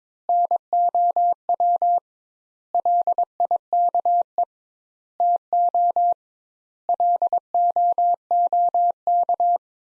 Morse code, heard as a keyed tone, about 22 wpm, 700 Hz. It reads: NOW LIKE TO LOOK
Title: Sentences Top 200 Words - 22wpm